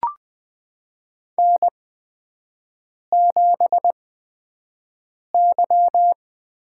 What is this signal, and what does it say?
Morse code, heard as a keyed tone, about 20 wpm, spaced as for 5 wpm, 700 Hz. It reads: N7Y
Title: USA Call Signs - 5 Characters - 8x Character Spacing - 20wpm